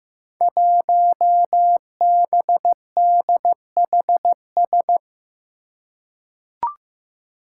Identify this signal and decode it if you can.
Morse code, heard as a keyed tone, about 15 wpm, 700 Hz. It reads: 1BDHS
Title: Five Challenging Letter-Number Combinations 15wpm